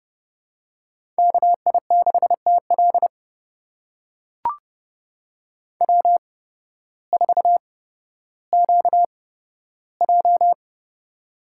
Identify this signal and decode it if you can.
Morse code, heard as a keyed tone, about 30 wpm, spaced as for 14 wpm, 700 Hz. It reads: KI6TL W4QJ
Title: USA Call Signs - 8x Character Spacing - 30wpm